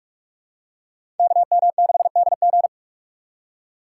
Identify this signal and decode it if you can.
Morse code, heard as a keyed tone, about 46 wpm, 700 Hz. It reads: KM6DG
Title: USA Call Signs - 4x Character Spacing - 45wpm